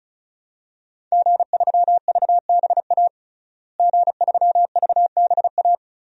Morse code, heard as a keyed tone, about 36 wpm, 700 Hz. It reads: G3VBA G3VBA
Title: DXCC-100 Call Signs - 5 Characters - Repeated 3x - 35wpm